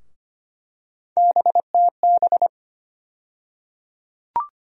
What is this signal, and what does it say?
Morse code, heard as a keyed tone, about 25 wpm, 700 Hz. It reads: BTB
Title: USA Call Sign Postfixes - 3 Letters 25wpm